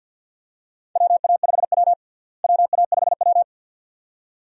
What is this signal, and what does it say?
Morse code, heard as a keyed tone, about 46 wpm, 700 Hz. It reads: WA5W WA5W
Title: USA Call Signs - 4 Characters - Repeated 2x - 50wpm